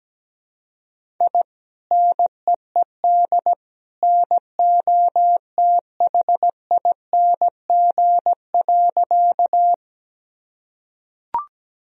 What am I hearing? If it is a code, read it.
Morse code, heard as a keyed tone, about 17 wpm, 700 Hz. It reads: I NEED NOTHING.
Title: Sentences from Top 1000 Words - Review Part 1 - 17wpm